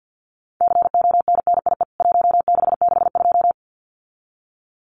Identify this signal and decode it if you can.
Morse code, heard as a keyed tone, about 48 wpm, 700 Hz. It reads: CONNIE 1662
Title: CWT Exchange 50wpm